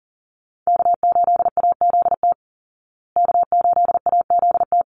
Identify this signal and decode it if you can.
Morse code, heard as a keyed tone, about 40 wpm, 700 Hz. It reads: K8AZT K8AZT
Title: Call Signs Repeated 3x - 40wpm